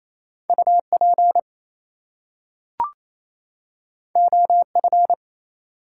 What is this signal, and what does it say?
Morse code, heard as a keyed tone, about 27 wpm, 700 Hz. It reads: UP OF
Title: Two Letter Words - 28wpm